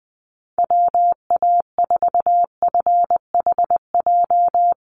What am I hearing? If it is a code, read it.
Morse code, heard as a keyed tone, about 20 wpm, 700 Hz. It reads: WA4FHJ
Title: USA Call Signs Repeated 2x 20wpm